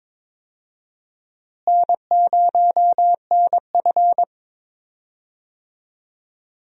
Morse code, heard as a keyed tone, about 21 wpm, 700 Hz. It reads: N0NF
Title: USA Call Signs - 4 Characters - Repeated 3x - 22wpm